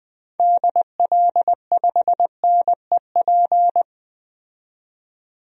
Morse code, heard as a keyed tone, about 20 wpm, 700 Hz. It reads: DL5NEP
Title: DXCC-100 Call Signs - 6 Characters - 20wpm